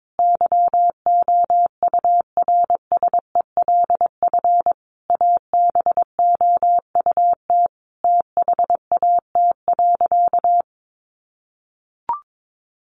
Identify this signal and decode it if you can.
Morse code, heard as a keyed tone, about 22 wpm, 700 Hz. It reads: YOURSELF ABOUT THAT.
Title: Sentences from Top 1200 Words 22wpm